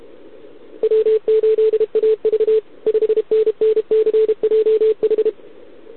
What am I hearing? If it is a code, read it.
Morse code, heard as a keyed tone, about 31 wpm, 440 Hz. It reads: W8AV 5NNCJH
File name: ARCHIV / OKDX2020 / OK1RIb